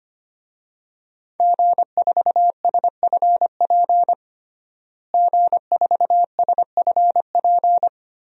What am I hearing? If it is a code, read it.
Morse code, heard as a keyed tone, about 25 wpm, 700 Hz. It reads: G4SFP G4SFP
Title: DXCC-100 Call Signs - 5 Characters - Repeated 2x - 25wpm